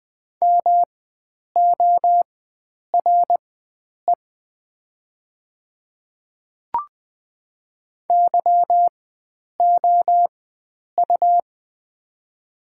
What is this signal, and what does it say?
Morse code, heard as a keyed tone, about 20 wpm, spaced as for 9 wpm, 700 Hz. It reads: MORE YOU
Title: 4x Character Spacing - Top 100 Words - 20wpm